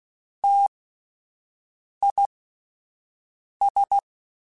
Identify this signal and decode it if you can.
Morse code, heard as a keyed tone, about 16 wpm, spaced as for 5 wpm, 800 Hz. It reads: TIS
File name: T Prefixes / T mp3s